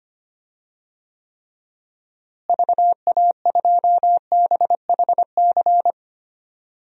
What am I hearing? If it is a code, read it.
Morse code, heard as a keyed tone, about 25 wpm, 700 Hz. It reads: VA2BHC
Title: DXCC-100 Call Signs - 6 Characters - Repeated 2x - 25wpm